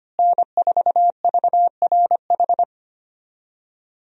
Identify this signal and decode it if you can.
Morse code, heard as a keyed tone, about 25 wpm, 700 Hz. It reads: N4VRH
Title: USA Call Signs - 5 Characters - Repeated 2x - 25wpm